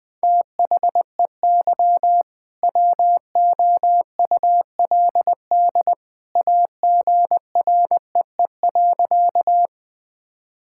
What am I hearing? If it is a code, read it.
Morse code, heard as a keyed tone, about 20 wpm, 700 Hz. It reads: THEY WOULD AGREE.
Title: Sentences with 2-Word Phrases - 20wpm